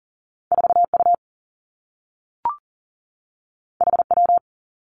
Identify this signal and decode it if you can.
Morse code, heard as a keyed tone, about 42 wpm, 700 Hz. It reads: 4 U H W
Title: Two Challenging Letter-Number Combinations 40wpm